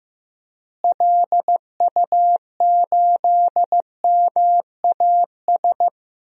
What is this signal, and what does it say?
Morse code, heard as a keyed tone, about 15 wpm, 700 Hz. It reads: LU8MAS
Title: DXCC-100 Call Signs - 6 Characters - Repeated 2x - 15wpm